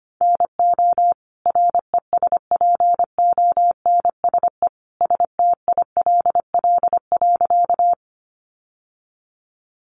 Morse code, heard as a keyed tone, about 25 wpm, 700 Hz. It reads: NO RESPONSE STILL.
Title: Sentences from Top 1400 Words 25wpm